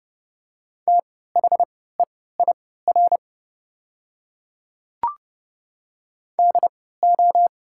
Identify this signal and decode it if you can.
Morse code, heard as a keyed tone, about 30 wpm, spaced as for 17 wpm, 700 Hz. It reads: THEIR DO
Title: 3x Character Spacing - Top 100 Words - 30wpm